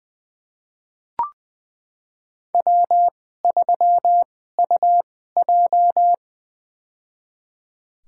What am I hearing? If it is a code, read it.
Morse code, heard as a keyed tone, about 21 wpm, spaced as for 15 wpm, 700 Hz. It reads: W3UJ
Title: USA Call Signs - 4 Characters - 2x Character Spacing - 20wpm